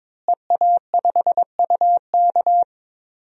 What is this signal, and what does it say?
Morse code, heard as a keyed tone, about 21 wpm, 700 Hz. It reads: EA5UK
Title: DXCC-100 Call Signs - 5 Characters - Repeated 3x - 22wpm